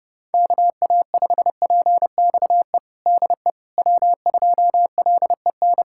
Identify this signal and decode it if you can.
Morse code, heard as a keyed tone, about 30 wpm, 700 Hz. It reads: KA5PXE DE W2LEN
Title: USA Callsign DE Callsign 30wpm